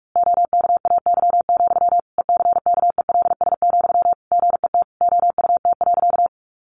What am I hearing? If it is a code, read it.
Morse code, heard as a keyed tone, about 44 wpm, 700 Hz. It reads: OKAY, ECKELS, GET OUT.
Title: A Sound Of Thunder 45wpm